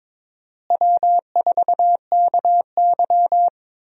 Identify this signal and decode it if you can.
Morse code, heard as a keyed tone, about 22 wpm, 700 Hz. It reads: W4KY
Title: USA Call Signs - 4 Characters - Repeated 2x - 22wpm